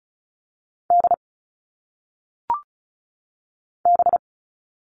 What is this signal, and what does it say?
Morse code, heard as a keyed tone, about 36 wpm, 700 Hz. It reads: D B
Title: All Cut Numbers - 1-Digit 35wpm